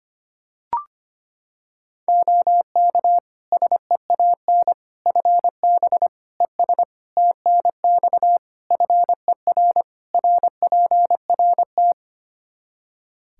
Morse code, heard as a keyed tone, about 25 wpm, 700 Hz. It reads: OK SEAN FB ES TNX FER RPRT
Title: Practice QSO - Exchange 2 - Info - 25wpm